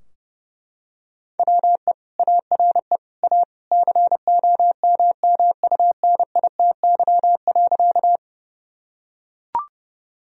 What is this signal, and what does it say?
Morse code, heard as a keyed tone, about 29 wpm, 700 Hz. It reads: WE ARE A COMMUNITY.
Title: Sentences from Top 1000 Words - Review Part 4 - 30wpm